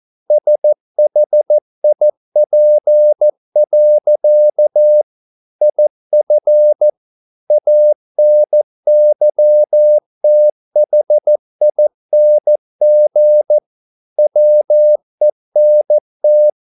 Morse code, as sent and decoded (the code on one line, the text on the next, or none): ... .... .. .--. .-.-.- / .. ..-. / .- -. -.-- - .... .. -. --. / .-- . -. -
SHIP. IF ANYTHING WENT